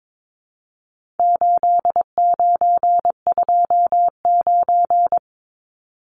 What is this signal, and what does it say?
Morse code, heard as a keyed tone, about 22 wpm, 700 Hz. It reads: 8929
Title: Numbers - 4-Digit 22wpm